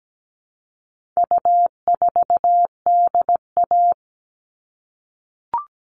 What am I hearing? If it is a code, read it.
Morse code, heard as a keyed tone, about 17 wpm, 700 Hz. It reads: U4DA
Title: All Cut Numbers - 4-Digit 17wpm